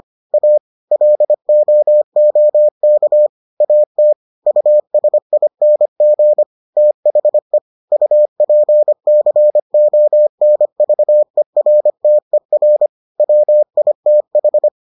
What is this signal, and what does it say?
Morse code, heard as a keyed tone, about 25 wpm, 600 Hz. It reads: A LOOK AT USING THE UPCONVERTER WITH